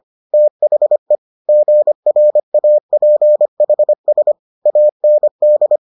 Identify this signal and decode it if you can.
Morse code, heard as a keyed tone, about 25 wpm, 600 Hz. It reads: THE GRAPHS AND